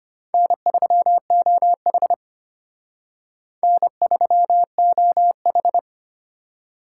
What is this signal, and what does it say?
Morse code, heard as a keyed tone, about 32 wpm, 700 Hz. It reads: N3OH N3OH